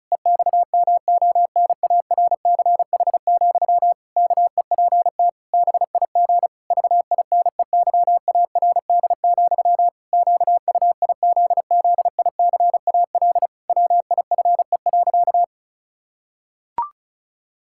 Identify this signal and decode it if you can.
Morse code, heard as a keyed tone, about 36 wpm, 700 Hz. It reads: EXMONARCH, KEPT BIG VINEYARD, QUIZZICAL WIFE.
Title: Pangram Challenge v2 - 35wpm